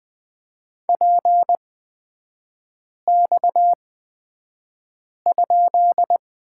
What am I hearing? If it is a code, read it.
Morse code, heard as a keyed tone, about 20 wpm, spaced as for 5 wpm, 700 Hz. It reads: PX?
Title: 255 Lesson 41e - Period - Three Characters 20wpm